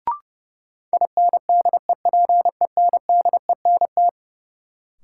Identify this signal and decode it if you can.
Morse code, heard as a keyed tone, about 30 wpm, 700 Hz. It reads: INDEPENDENT
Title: Sentences from Top 900 Words - New Word Drill - 30wpm